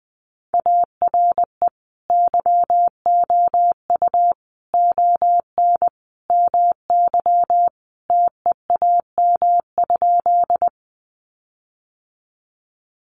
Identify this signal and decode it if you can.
Morse code, heard as a keyed tone, about 20 wpm, 700 Hz. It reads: ARE YOU ON MY TEAM?
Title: Sentences from Top 1500 Words 20wpm